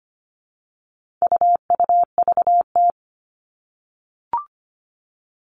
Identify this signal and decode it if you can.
Morse code, heard as a keyed tone, about 25 wpm, 700 Hz. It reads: UUVT
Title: All Cut Numbers - 4-Digit 25wpm